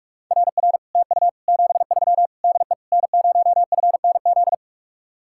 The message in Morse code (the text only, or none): RR TU 73 DE N0FNZ